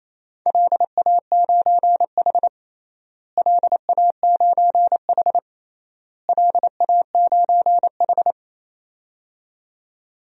Morse code, heard as a keyed tone, about 28 wpm, 700 Hz. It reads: LA9H LA9H LA9H
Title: DXCC-100 Call Signs - 4 Characters - Repeated 3x - 28wpm